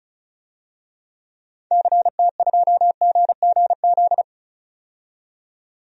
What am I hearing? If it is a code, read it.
Morse code, heard as a keyed tone, about 34 wpm, 700 Hz. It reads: CT2GGZ
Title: DXCC-100 Call Signs - 6 Characters - 35wpm